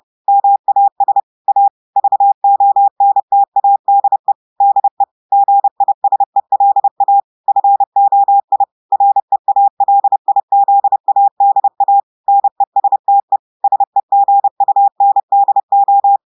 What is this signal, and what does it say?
Morse code, heard as a keyed tone, about 30 wpm, 800 Hz. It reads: MAS A VONTADE DE GISELA FOI REALIZADA NESTE SEGUNDO
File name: audios / portugais / portugais texte en clair 1 vitesse 30 mots minutes